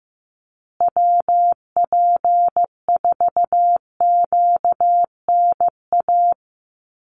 Morse code, heard as a keyed tone, about 15 wpm, 700 Hz. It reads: WP4QNA
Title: USA Call Signs Repeated 3x 15wpm